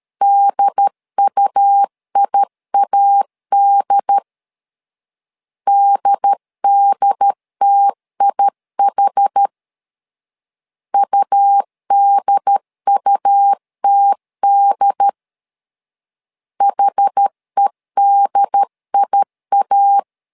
Morse code, heard as a keyed tone, about 13 wpm, 800 Hz. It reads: DUIAD DDTIH UDUTD HEDIA
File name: LEZIONI / 2 UHDN / 2 UHD 12wpm 1540